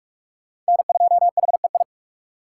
Morse code, heard as a keyed tone, about 44 wpm, 700 Hz. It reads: N2HEI